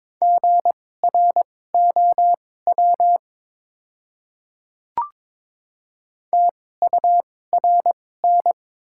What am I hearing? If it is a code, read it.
Morse code, heard as a keyed tone, about 22 wpm, spaced as for 16 wpm, 700 Hz. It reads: GROW TURN
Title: 2x Character Spacing - Top 300 Words - 22wpm